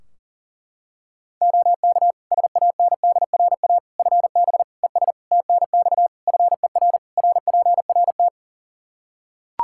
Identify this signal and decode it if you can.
Morse code, heard as a keyed tone, about 40 wpm, 700 Hz. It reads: OK SANDRA FB ES TNX FER RPRT
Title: Practice QSO - Exchange 2 - Info - 40wpm